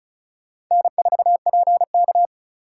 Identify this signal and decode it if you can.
Morse code, heard as a keyed tone, about 40 wpm, 700 Hz. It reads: N4PK